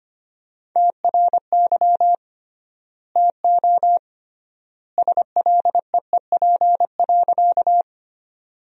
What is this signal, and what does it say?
Morse code, heard as a keyed tone, about 25 wpm, 700 Hz. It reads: TRY TO SLEEP.
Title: 3x Word-Spacing - Sentences from Top 500 Words - 25wpm